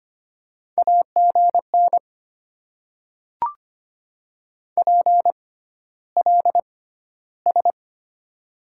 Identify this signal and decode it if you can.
Morse code, heard as a keyed tone, about 25 wpm, spaced as for 13 wpm, 690 Hz. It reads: AGN PLS
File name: Top 90 QSO Elements - 6x Character Spacing - 25wpm